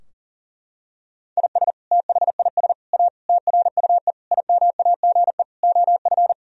Morse code, heard as a keyed tone, about 40 wpm, 700 Hz. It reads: IS THIS A TRUE IMAGE OF
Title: Sentences from Top 1000 Words - Review Part 4 - 40wpm